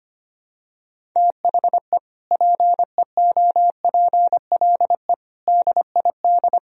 Morse code, heard as a keyed tone, about 25 wpm, 700 Hz. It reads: THE PEOPLE DID
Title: Sentences from Top 1000 Words - Review Part 1 - 25wpm